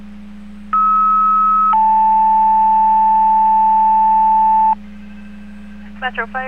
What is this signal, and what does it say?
Two-tone page, heard as a one-second tone, then a three-second tone, about 1290 then 880 Hz, with speech. It is heard from 0.7 to 4.7 s.